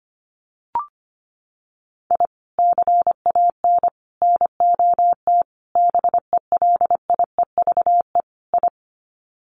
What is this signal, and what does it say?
Morse code, heard as a keyed tone, about 25 wpm, 700 Hz. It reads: I CAN NOT BELIEVE I
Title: Sets of 5 Words Encore2 25wpm